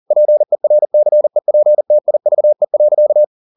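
Morse code, heard as a keyed tone, about 40 wpm, 600 Hz. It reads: PERCEPTIVE.
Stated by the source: Morse code at 40 wpm, practice text, CW med Gnister.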